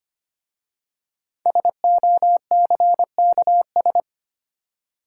Morse code, heard as a keyed tone, about 25 wpm, 700 Hz. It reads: SOCKS
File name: practice / 25wpm / Easy Words with Hints - 25wpm